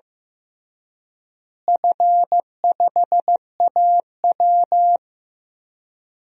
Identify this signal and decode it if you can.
Morse code, heard as a keyed tone, about 15 wpm, 700 Hz. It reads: F5AW
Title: DXCC-100 Call Signs - 4 Characters - Repeated 3x - 15wpm